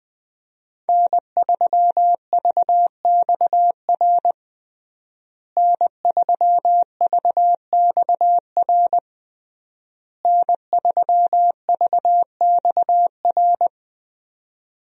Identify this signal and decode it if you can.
Morse code, heard as a keyed tone, about 20 wpm, 700 Hz. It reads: N3VXR N3VXR N3VXR
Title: USA Call Signs - 5 Characters - Repeated 3x - 20wpm